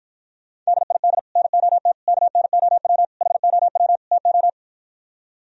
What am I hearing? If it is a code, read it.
Morse code, heard as a keyed tone, about 58 wpm, 700 Hz. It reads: DID NOT KNOW HOW TO